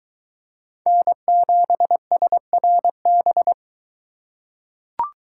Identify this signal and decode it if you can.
Morse code, heard as a keyed tone, about 22 wpm, 700 Hz. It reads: N7SRB